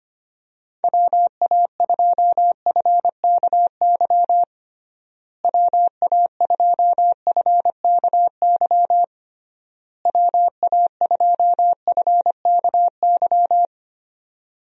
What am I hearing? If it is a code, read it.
Morse code, heard as a keyed tone, about 25 wpm, 700 Hz. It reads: WA2FKY WA2FKY WA2FKY
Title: USA Call Signs - 6 Characters - Repeated 3x - 25wpm